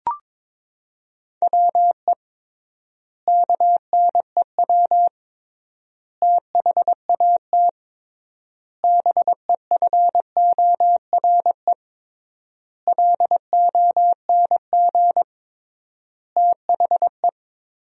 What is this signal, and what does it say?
Morse code, heard as a keyed tone, about 22 wpm, 700 Hz. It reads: WE KNEW THAT BEFORE LONG THE
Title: 3x Word-Spacing - Sentences from Top 400 Words - 22wpm